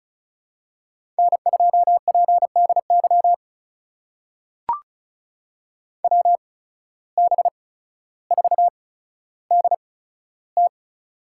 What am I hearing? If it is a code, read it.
Morse code, heard as a keyed tone, about 36 wpm, spaced as for 15 wpm, 700 Hz. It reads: N2PDY WB4DT
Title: USA Call Signs - 5 Characters - 8x Character Spacing - 35wpm